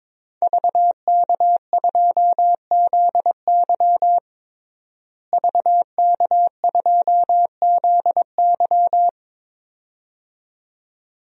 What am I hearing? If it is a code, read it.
Morse code, heard as a keyed tone, about 22 wpm, 700 Hz. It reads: VK2ZY VK2ZY
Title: DXCC-100 Call Signs - 5 Characters - Repeated 2x - 22wpm